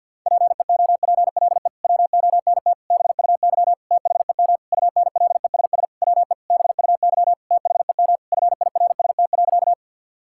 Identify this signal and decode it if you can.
Morse code, heard as a keyed tone, about 50 wpm, 700 Hz. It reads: PEOPLE WONT BUY THEM UNLESS WE BUY THEM FIRST.
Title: Sentences from Top 1000 Words - Review Part 6 - 50wpm